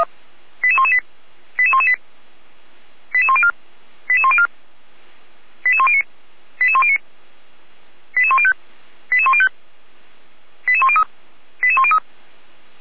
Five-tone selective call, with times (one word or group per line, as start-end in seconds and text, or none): 0.6-1.0 88118
1.6-1.9 88118
3.1-3.5 88104
4.1-4.5 88104
5.7-6.0 88109
6.6-7.0 88109
8.2-8.5 88105
9.1-9.5 88105
10.7-11.0 88103
11.6-12.0 88103